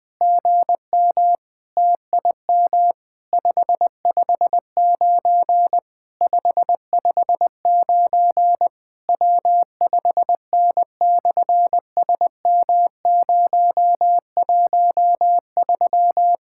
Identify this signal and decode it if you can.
Morse code, heard as a keyed tone, about 20 wpm, 700 Hz. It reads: GM TIM 559 559 W5N/SM013
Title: SOTA QSO Practice 20wpm